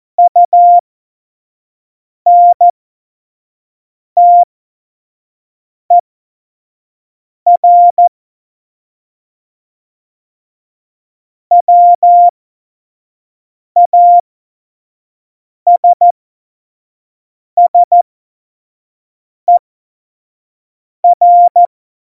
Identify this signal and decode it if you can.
Morse code, heard as a keyed tone, about 14 wpm, spaced as for 5 wpm, 700 Hz. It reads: UNTER WASSER
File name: Tempo 25 F70